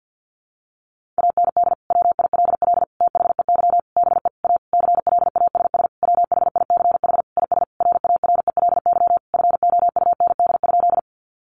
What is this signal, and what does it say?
Morse code, heard as a keyed tone, about 50 wpm, 700 Hz. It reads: AND WILL THEY BE A CLASS WHICH IS RARELY FOUND?
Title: Sentences from Top 1400 Words 50wpm